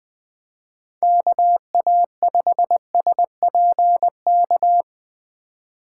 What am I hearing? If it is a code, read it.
Morse code, heard as a keyed tone, about 20 wpm, 700 Hz. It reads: KA5SPK